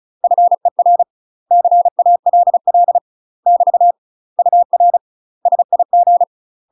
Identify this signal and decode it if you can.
Morse code, heard as a keyed tone, about 36 wpm, 700 Hz. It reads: FER CALL <BT> UR SIG
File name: UK MASTER QSOs TRAINING / 35WPM / UK MASTER QSOs 0023